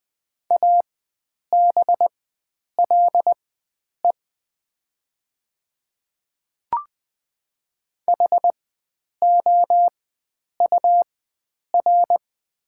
Morse code, heard as a keyed tone, about 20 wpm, spaced as for 9 wpm, 700 Hz. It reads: ABLE HOUR
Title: 4x Character Spacing - Top 500 Words - 20wpm